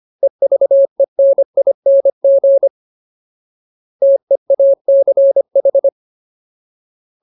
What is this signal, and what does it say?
Morse code, heard as a keyed tone, about 25 wpm, 550 Hz. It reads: EVENING TEACH
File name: Ultimate Word List / 25WPM Ultimate